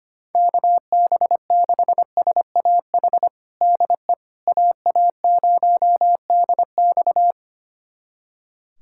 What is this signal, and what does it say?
Morse code, heard as a keyed tone, about 25 wpm, 700 Hz. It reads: KB6SAH DE AA0DX
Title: USA Callsign DE Callsign 25wpm